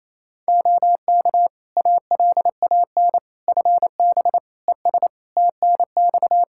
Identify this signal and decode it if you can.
Morse code, heard as a keyed tone, about 28 wpm, 700 Hz. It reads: OK ALAN FB ES TNX
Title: Practice QSO - Exchange 2 - Info - 28wpm